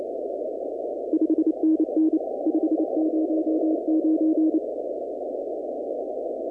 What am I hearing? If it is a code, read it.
Morse code, heard as a keyed tone, about 29 wpm, 320 Hz. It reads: 5NN 509
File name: contests / 2019-cw-wpx-kt5j-k5pi- -2330 0-20